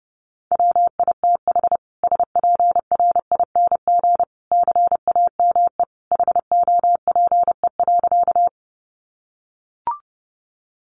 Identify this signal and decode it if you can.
Morse code, heard as a keyed tone, about 30 wpm, 700 Hz. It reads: WITH SPRING CAME HOPE.
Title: Sentences from Top 1200 Words 30wpm